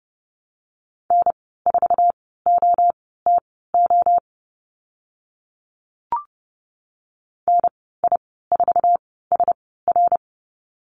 Morse code, heard as a keyed tone, about 30 wpm, spaced as for 17 wpm, 700 Hz. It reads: N4OTO NI4SR
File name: Lesson 14g - Number 2 - Call Signs 30wpm